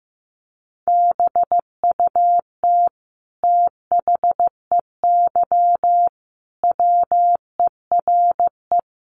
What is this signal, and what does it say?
Morse code, heard as a keyed tone, about 15 wpm, 700 Hz. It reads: BUT THEY WERE
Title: Sentences from Top 1500 Words 15wpm